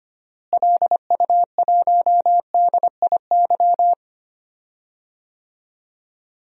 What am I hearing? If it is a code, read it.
Morse code, heard as a keyed tone, about 25 wpm, 700 Hz. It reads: LU1DIY
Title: DXCC-100 Call Signs - 6 Characters - Repeated 3x - 25wpm